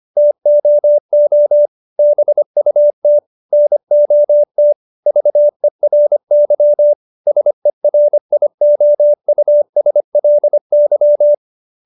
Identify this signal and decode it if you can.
Morse code, heard as a keyed tone, about 25 wpm, 590 Hz. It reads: TOO BUT NOT VERY SERIOUSLY
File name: QSO315 25